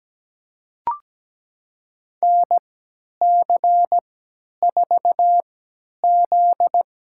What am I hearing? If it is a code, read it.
Morse code, heard as a keyed tone, about 17 wpm, spaced as for 10 wpm, 700 Hz. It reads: NC4Z
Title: USA Call Signs - 4 Characters - 3x Character Spacing - 17wpm